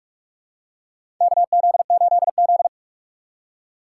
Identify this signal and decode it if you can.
Morse code, heard as a keyed tone, about 46 wpm, 690 Hz. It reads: KZ8Z
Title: USA Call Signs - 4 Characters - 3x Character Spacing - 45wpm